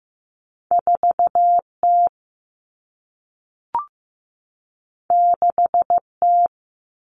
Morse code, heard as a keyed tone, about 15 wpm, 700 Hz. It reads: 4T 6T
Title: All Cut Numbers - 2-Digit 15wpm